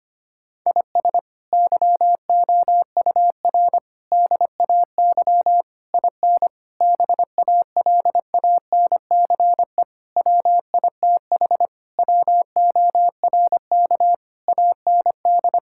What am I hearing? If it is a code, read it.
Morse code, heard as a keyed tone, about 25 wpm, 700 Hz. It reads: IS YOUR DAY IN BALANCE WITH WORK AND
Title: Sentences from Top 1000 Words - Review Part 6 - 25wpm